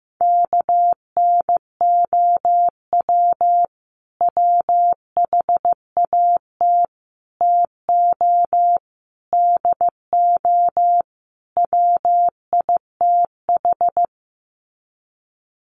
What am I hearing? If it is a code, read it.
Morse code, heard as a keyed tone, about 15 wpm, 700 Hz. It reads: KNOW WHAT TO DO WITH
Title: Sets of 5 Words 15wpm